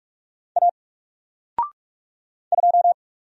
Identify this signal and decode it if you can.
Morse code, heard as a keyed tone, about 44 wpm, 700 Hz. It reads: A 2
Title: Single Character (Letter:Number:Punctuation:Prosign) Alt 45wpm